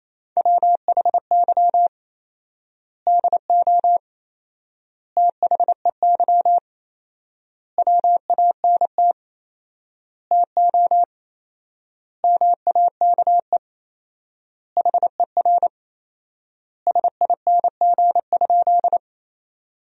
Morse code, heard as a keyed tone, about 28 wpm, 700 Hz. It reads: WHY DO THEY WANT TO MAKE HER SING?
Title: 4x Word-Spacing - Sentences from Top 400 Words - 28wpm